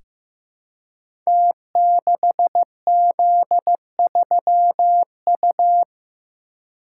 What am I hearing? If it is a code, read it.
Morse code, heard as a keyed tone, about 15 wpm, 700 Hz. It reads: T6Z3U
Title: Code Groups - 5 Letter:Numbers - 15wpm